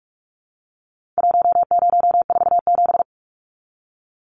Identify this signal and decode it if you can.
Morse code, heard as a keyed tone, about 44 wpm, 700 Hz. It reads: 1047
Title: Numbers - Contest Style - 4-Digit 45wpm